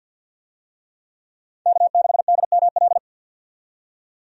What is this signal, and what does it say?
Morse code, heard as a keyed tone, about 49 wpm, 700 Hz. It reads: K6DML
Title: USA Call Signs - 5 Characters - 50wpm